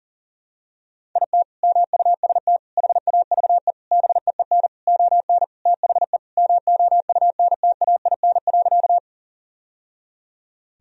Morse code, heard as a keyed tone, about 37 wpm, 700 Hz. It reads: IT MUST HAVE BEEN ON THE MOUNTAIN.
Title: Sentences from Top 1000 Words - Review Part 1 - 40wpm